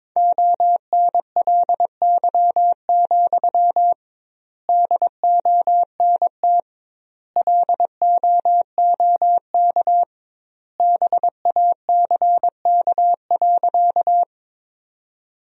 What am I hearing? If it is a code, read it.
Morse code, heard as a keyed tone, about 22 wpm, 700 Hz. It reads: ONLY, DONT LOOK BACK.
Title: 2x Word-Spacing - Sentences from Top 300 Words - 22wpm